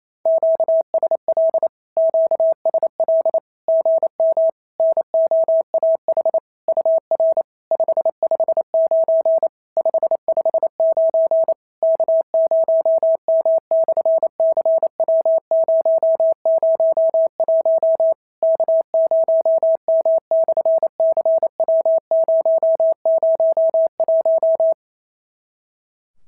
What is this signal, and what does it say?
Morse code, heard as a keyed tone, about 28 wpm, 650 Hz. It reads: QSL QSL GM NOAH UR 559 559 K0M/CW001 K0M/CW001
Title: SOTA QSO Practice 28wpm